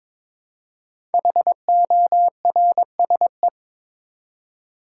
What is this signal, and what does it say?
Morse code, heard as a keyed tone, about 22 wpm, 700 Hz. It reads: HORSE
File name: 5 Letter Words - First 3 Letters Given - 22wpm